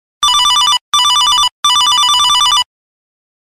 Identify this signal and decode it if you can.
There is no code.